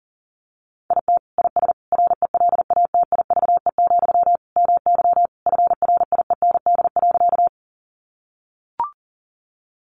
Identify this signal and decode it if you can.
Morse code, heard as a keyed tone, about 40 wpm, 700 Hz. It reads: IT IS RELATIVE, MY FRIEND.
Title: Sentences from Top 1400 Words 40wpm